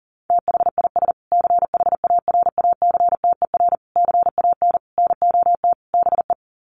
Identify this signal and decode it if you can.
Morse code, heard as a keyed tone, about 40 wpm, 700 Hz. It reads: THIS CHARACTER CAN NOT BE
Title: Sets of 5 Words Encore2 40wpm